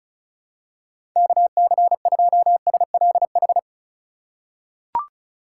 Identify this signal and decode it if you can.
Morse code, heard as a keyed tone, about 34 wpm, 700 Hz. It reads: KC2SLH
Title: USA Call Signs - 6 Characters - 4x Character Spacing - 35wpm